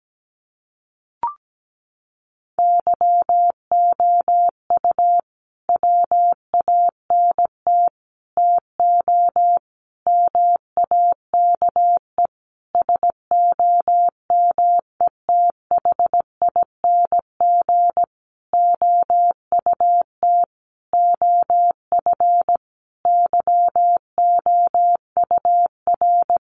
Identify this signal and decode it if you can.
Morse code, heard as a keyed tone, about 17 wpm, 700 Hz. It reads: YOU WANT TO MAKE SOMETHING OUT OF YOUR
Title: Sentences from Top 600 Words 17wpm